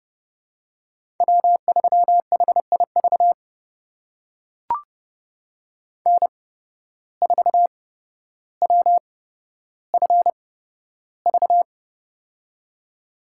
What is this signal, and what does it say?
Morse code, heard as a keyed tone, about 32 wpm, spaced as for 13 wpm, 700 Hz. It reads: W3HIV N4WFV
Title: USA Call Signs - 5 Characters - 8x Character Spacing - 30wpm